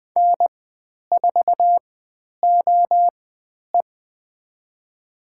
Morse code, heard as a keyed tone, about 20 wpm, spaced as for 10 wpm, 700 Hz. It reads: N4OE
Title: 095 Lesson 16g - Letter c - Call Signs 20wpm